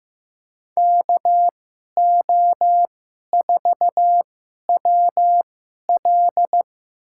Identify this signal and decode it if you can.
Morse code, heard as a keyed tone, about 16 wpm, spaced as for 11 wpm, 700 Hz. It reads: KO4WL
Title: USA Call Signs - 5 Characters - 2x Character Spacing - 15wpm